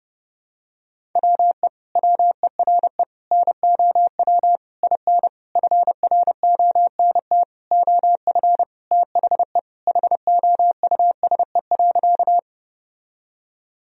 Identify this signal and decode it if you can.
Morse code, heard as a keyed tone, about 32 wpm, 700 Hz. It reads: WE WERE NOW IN FRONT OF THE HOUSE.
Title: Sentences from Top 1000 Words - Review Part 2 - 30wpm